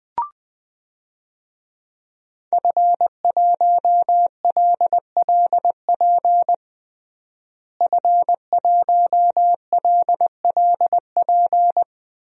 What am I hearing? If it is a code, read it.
Morse code, heard as a keyed tone, about 20 wpm, 700 Hz. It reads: F1LLP F1LLP
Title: DXCC-100 Call Signs - 5 Characters - Repeated 2x - 20wpm